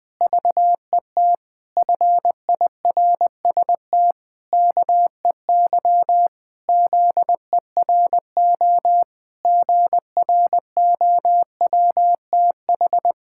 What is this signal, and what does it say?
Morse code, heard as a keyed tone, about 21 wpm, 700 Hz. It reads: VET FIRST KEY ZERO GROWTH
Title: Pangram Challenge v2 - 20wpm